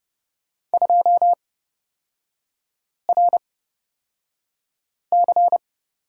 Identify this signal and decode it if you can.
Morse code, heard as a keyed tone, about 30 wpm, spaced as for 5 wpm, 700 Hz. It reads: 2RC
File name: Lesson 36e - BT - Three Characters 30wpm